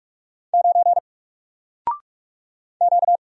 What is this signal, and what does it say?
Morse code, heard as a keyed tone, about 44 wpm, 700 Hz. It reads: 9 Q
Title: Single Letter-Number Alt 45wpm